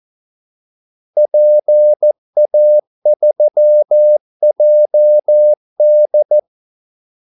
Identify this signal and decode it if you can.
Morse code, heard as a keyed tone, about 14 wpm, 600 Hz. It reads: PA3JD
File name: Callsigns-14wpm